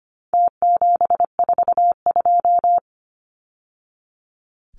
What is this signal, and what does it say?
Morse code, heard as a keyed tone, about 25 wpm, 700 Hz. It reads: T742
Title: Numbers - Contest Style - 4-Digit 25wpm